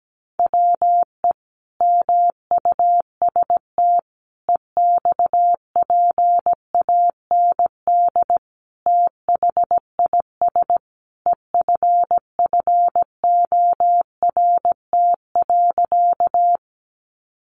Morse code, heard as a keyed tone, about 17 wpm, 700 Hz. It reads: WE MUST EXPAND THIS EFFORT.
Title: Sentences from Top 1400 Words 17wpm